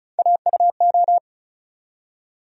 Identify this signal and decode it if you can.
Morse code, heard as a keyed tone, about 36 wpm, 700 Hz. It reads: AUO
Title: USA Call Sign Postfixes - 3 Letters 35wpm